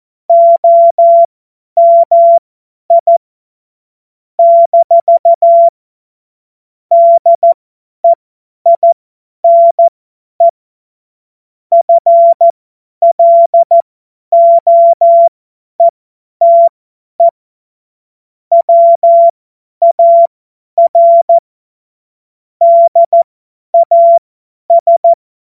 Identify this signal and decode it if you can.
Morse code, heard as a keyed tone, about 14 wpm, spaced as for 10 wpm, 680 Hz. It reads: OMI - DEINE FLOETE WAR DAS
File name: Morsekurs / Witze / Witz 31. Januar 2023 50 F70